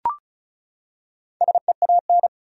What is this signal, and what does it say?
Morse code, heard as a keyed tone, about 34 wpm, 700 Hz. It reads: SEAN
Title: USA Common Names - 35wpm